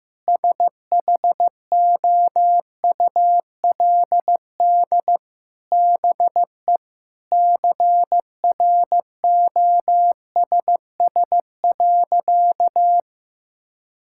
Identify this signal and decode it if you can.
Morse code, heard as a keyed tone, about 15 wpm, 700 Hz. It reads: SHOULD BE CROSS.
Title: Sentences from Top 1000 Words - Review Part 6 - 15wpm